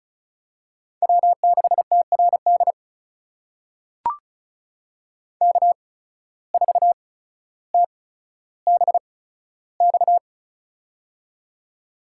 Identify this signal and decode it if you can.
Morse code, heard as a keyed tone, about 36 wpm, spaced as for 15 wpm, 700 Hz. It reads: W6TRD K4TBX
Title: USA Call Signs - 5 Characters - 8x Character Spacing - 35wpm